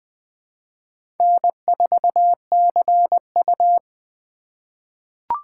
Morse code, heard as a keyed tone, about 20 wpm, 700 Hz. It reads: N4CU
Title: USA Call Signs - 4 Characters - 6x Character Spacing - 20wpm